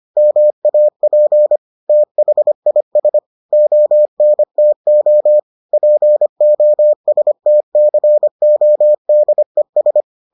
Morse code, heard as a keyed tone, about 25 wpm, 600 Hz. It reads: MAP THIS ONTO POSTCODES